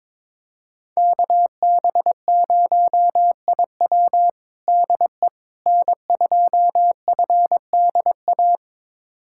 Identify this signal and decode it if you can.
Morse code, heard as a keyed tone, about 22 wpm, 700 Hz. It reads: KB0IW DE N2FDA
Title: USA Callsign DE Callsign 22wpm